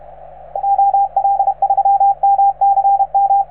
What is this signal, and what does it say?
Morse code, heard as a keyed tone, about 35 wpm, 790 Hz, with radio noise. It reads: OL3MCM